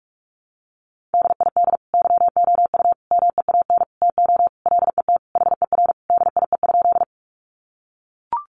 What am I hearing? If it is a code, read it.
Morse code, heard as a keyed tone, about 44 wpm, 700 Hz. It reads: DID YOU MEAN TO LET HER DIE?
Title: Sentences from Top 1300 Words 45wpm